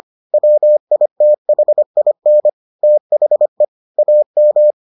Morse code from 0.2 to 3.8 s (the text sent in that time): WITHIN THE